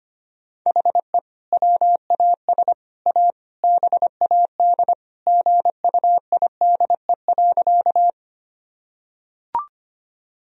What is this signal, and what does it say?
Morse code, heard as a keyed tone, about 25 wpm, 700 Hz. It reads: HE WAS A BAD GUIDE.
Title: Sentences from Top 1000 Words - Review Part 3 - 25wpm